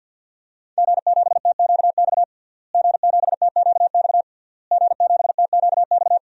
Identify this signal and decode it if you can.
Morse code, heard as a keyed tone, about 51 wpm, 700 Hz. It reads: G7TQX G7TQX G7TQX
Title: DXCC-100 Call Signs - 5 Characters - Repeated 3x - 50wpm